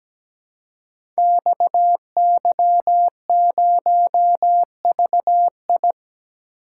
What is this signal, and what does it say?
Morse code, heard as a keyed tone, about 17 wpm, 700 Hz. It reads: XY0VI
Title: Code Groups - 5 Letter:Numbers - 17wpm